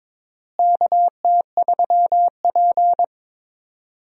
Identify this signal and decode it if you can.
Morse code, heard as a keyed tone, about 22 wpm, 700 Hz. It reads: KT3P
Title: USA Call Signs - 4 Characters - Repeated 2x - 22wpm